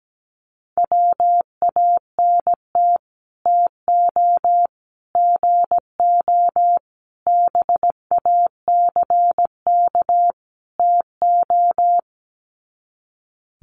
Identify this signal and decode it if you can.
Morse code, heard as a keyed tone, about 17 wpm, 700 Hz. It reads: WANT TO GO BACK TO
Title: Sets of 5 Words 17wpm